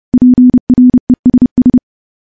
Morse code, heard as a keyed tone, about 30 wpm, 250 Hz. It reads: PRESS
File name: morse / morse-training-30WPM